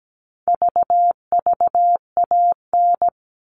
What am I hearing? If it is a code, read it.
Morse code, heard as a keyed tone, about 17 wpm, 700 Hz. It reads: VVAN
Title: All Cut Numbers - 4-Digit 17wpm